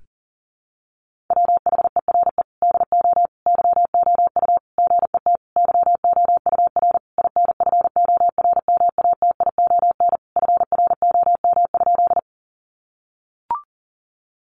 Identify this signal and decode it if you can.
Morse code, heard as a keyed tone, about 40 wpm, 700 Hz. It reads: WHERE DO YOU GET YOUR INFORMATION FROM?
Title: Sentences from Top 600 Words 40wpm